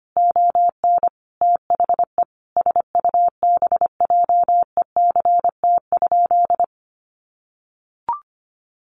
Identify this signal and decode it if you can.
Morse code, heard as a keyed tone, about 25 wpm, 700 Hz. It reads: ON THE SUBJECT?
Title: Sentences from Top 800 Words 25wpm